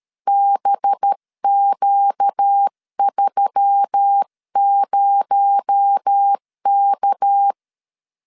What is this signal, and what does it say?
Morse code, heard as a keyed tone, about 13 wpm, 800 Hz. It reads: BQ30K
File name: CW / DET / 1 6 1500C S C 12wpm 3533